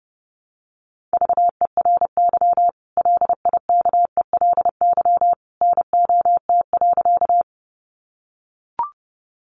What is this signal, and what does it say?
Morse code, heard as a keyed tone, about 30 wpm, 700 Hz. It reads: VERY LIKELY NOT.
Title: Sentences from Top 600 Words 30wpm